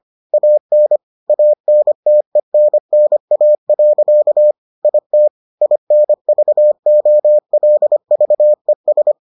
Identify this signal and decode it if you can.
Morse code, heard as a keyed tone, about 25 wpm, 600 Hz. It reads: AN ANTENNA. IT INVOLVES